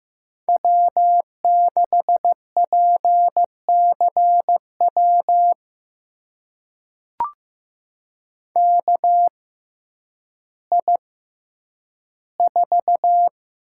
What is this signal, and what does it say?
Morse code, heard as a keyed tone, about 15 wpm, spaced as for 9 wpm, 700 Hz. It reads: W6PCW KI4
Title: USA Call Signs - 6x Character Spacing - 15wpm